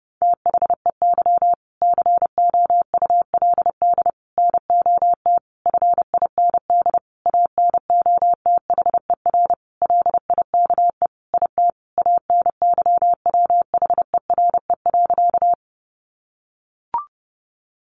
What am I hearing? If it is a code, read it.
Morse code, heard as a keyed tone, about 30 wpm, 700 Hz. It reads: THEY COULD NOT FIND ANOTHER LIKE IT ANYWHERE.
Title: Sentences from Top 1400 Words 30wpm